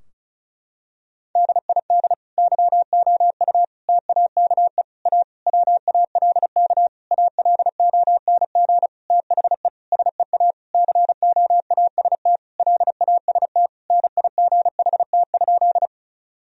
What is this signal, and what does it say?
Morse code, heard as a keyed tone, about 36 wpm, 700 Hz. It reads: DID YOU TAKE A WALK ALONG THE SEA COAST LAST NIGHT?
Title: Sentences from Top 1000 Words - Review Part 4 - 35wpm